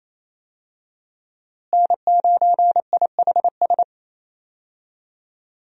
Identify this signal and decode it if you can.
Morse code, heard as a keyed tone, about 28 wpm, 700 Hz. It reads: N9IHS
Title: USA Call Signs - 5 Characters - Repeated 3x - 28wpm